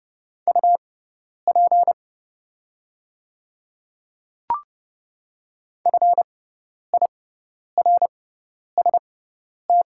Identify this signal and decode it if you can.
Morse code, heard as a keyed tone, about 29 wpm, spaced as for 10 wpm, 700 Hz. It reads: UP FIRST
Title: 6x Character Spacing - Top 100 Words - 30wpm